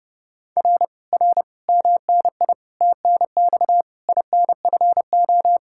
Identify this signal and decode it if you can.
Morse code, heard as a keyed tone, about 31 wpm, 700 Hz. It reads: R R MNI TNX INFO